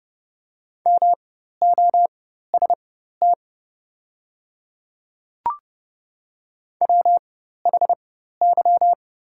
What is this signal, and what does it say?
Morse code, heard as a keyed tone, about 30 wpm, spaced as for 14 wpm, 700 Hz. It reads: MOST WHY
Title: 4x Character Spacing - Top 200 Words - 30wpm